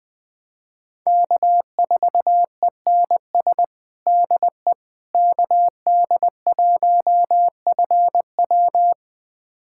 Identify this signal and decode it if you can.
Morse code, heard as a keyed tone, about 20 wpm, 700 Hz. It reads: K4ENS DE KD1FW
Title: USA Callsign DE Callsign 20wpm